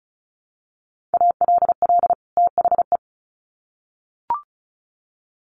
Morse code, heard as a keyed tone, about 34 wpm, 700 Hz. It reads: ALL THE
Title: Sets of 2 Words 35wpm